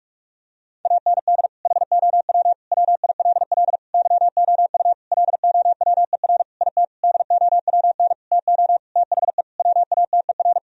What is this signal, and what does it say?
Morse code, heard as a keyed tone, about 47 wpm, 700 Hz. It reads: AND HOW WILL YOU LOWER IT DOWN TO THE WATER
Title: Sentences from Top 1000 Words - Review Part 4 - 45wpm